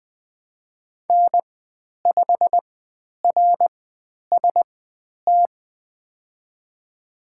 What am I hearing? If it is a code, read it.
Morse code, heard as a keyed tone, about 20 wpm, spaced as for 10 wpm, 700 Hz. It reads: N5RST
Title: 095 Lesson 16g - Letter c - Call Signs 20wpm